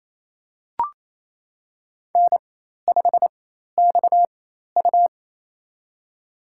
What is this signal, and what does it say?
Morse code, heard as a keyed tone, about 28 wpm, spaced as for 13 wpm, 700 Hz. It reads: N5XU
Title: USA Call Signs - 4 Characters - 4x Character Spacing - 28wpm